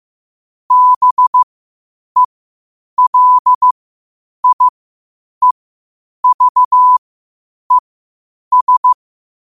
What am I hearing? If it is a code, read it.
Morse code, heard as a keyed tone, about 15 wpm, spaced as for 8 wpm, 1000 Hz. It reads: BELIEVES
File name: qotdfiles / 8wpm